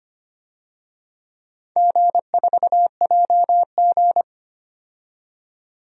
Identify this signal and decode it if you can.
Morse code, heard as a keyed tone, about 25 wpm, 700 Hz. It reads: G4JG
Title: DXCC-100 Call Signs - 4 Characters - 25wpm